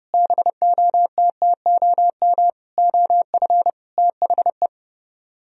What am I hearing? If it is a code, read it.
Morse code, heard as a keyed tone, about 35 wpm, 700 Hz. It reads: BOTTOM OF THE